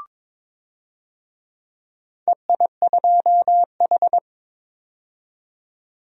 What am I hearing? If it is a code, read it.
Morse code, heard as a keyed tone, about 21 wpm, 700 Hz. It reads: EI2H
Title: DXCC-100 Call Signs - 4 Characters - 22wpm